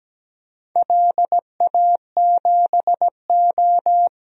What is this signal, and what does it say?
Morse code, heard as a keyed tone, about 17 wpm, 700 Hz. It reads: LA7O
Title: DXCC-100 Call Signs - 4 Characters - 17wpm